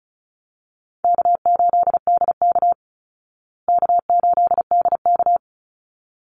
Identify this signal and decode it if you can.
Morse code, heard as a keyed tone, about 34 wpm, 700 Hz. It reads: K8DK K8DK
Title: Easy Call Signs Repeated 2x - 35wpm